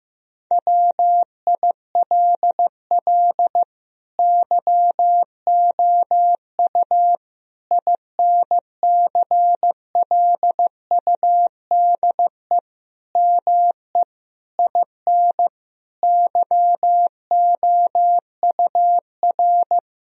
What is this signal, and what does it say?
Morse code, heard as a keyed tone, about 16 wpm, 700 Hz. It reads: WILL YOU INCLUDE ME IN YOUR
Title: Sentences from Top 1000 Words - Review Part 3 - 15wpm